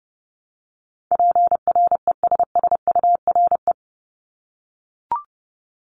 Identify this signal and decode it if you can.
Morse code, heard as a keyed tone, about 30 wpm, 700 Hz. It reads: PRESSURE
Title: Multi-letter Phonemes Repeated 3x 30wpm